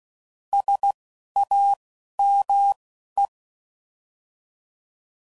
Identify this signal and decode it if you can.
Morse code, heard as a keyed tone, about 16 wpm, 800 Hz. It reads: S A M E